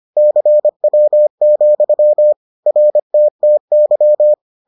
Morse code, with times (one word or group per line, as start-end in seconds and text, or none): 0.2-2.3 CW,
2.7-4.3 RTTY